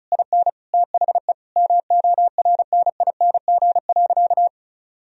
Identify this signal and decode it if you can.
Morse code, heard as a keyed tone, about 34 wpm, 700 Hz. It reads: IN THE MORNING.
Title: Sentences from Top 1000 Words - Review Part 4 - 35wpm